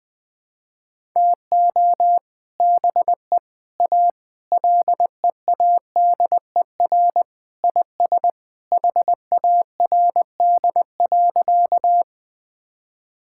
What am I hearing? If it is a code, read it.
Morse code, heard as a keyed tone, about 20 wpm, 700 Hz. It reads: TO BE A LEADER IS HARD.
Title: Sentences with 3-Word Phrases - 20wpm